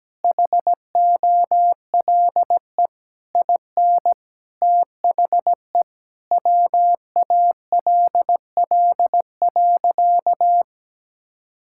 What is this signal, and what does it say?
Morse code, heard as a keyed tone, about 17 wpm, 700 Hz. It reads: HOLE IN THE WALL.
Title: Sentences from Top 1600 Words 17wpm